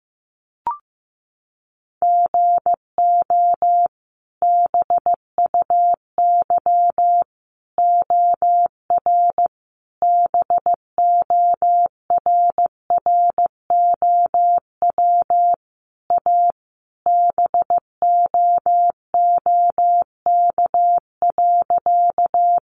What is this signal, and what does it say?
Morse code, heard as a keyed tone, about 15 wpm, 700 Hz. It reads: GO BUY OR BORROW A BOOK.
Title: Sentences from Top 1500 Words 15wpm